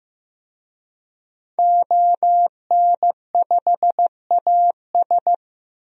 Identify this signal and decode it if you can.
Morse code, heard as a keyed tone, about 15 wpm, 700 Hz. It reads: ON5AS
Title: DXCC-100 Call Signs - 5 Characters - 15wpm